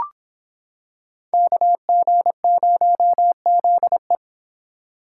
Morse code, heard as a keyed tone, about 26 wpm, 700 Hz. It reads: KG0ZE